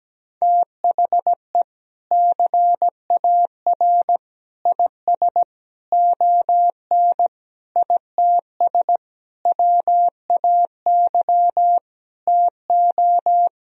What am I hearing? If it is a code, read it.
Morse code, heard as a keyed tone, about 17 wpm, 700 Hz. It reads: THE CAR IS ON ITS WAY TO
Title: Sentences from Top 1000 Words - Review Part 1 - 17wpm